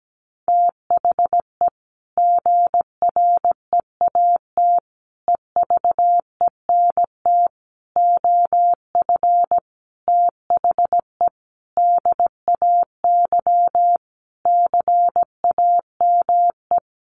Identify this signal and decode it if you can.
Morse code, heard as a keyed tone, about 17 wpm, 700 Hz. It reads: THE GREAT EVENT OF THE DAY CAME
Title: Sentences from Top 1500 Words 17wpm